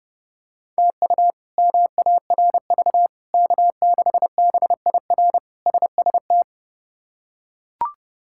Morse code, heard as a keyed tone, about 30 wpm, 700 Hz. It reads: TU MARV K6BIR SST
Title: K1USN SST QSO Practice - 30wpm